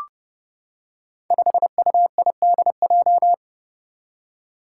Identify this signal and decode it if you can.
Morse code, heard as a keyed tone, about 31 wpm, 700 Hz. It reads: 5UIDJ
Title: Five Challenging Letter-Number Combinations 30wpm